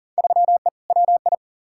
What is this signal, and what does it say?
Morse code, heard as a keyed tone, about 40 wpm, 700 Hz. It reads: 3E WI
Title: Field Day Exchanges Encore - 40wpm